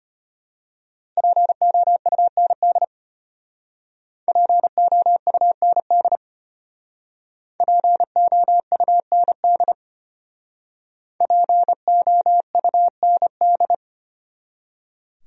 Morse code, heard as a keyed tone, about 33 wpm, 700 Hz. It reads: POUND POUND POUND POUND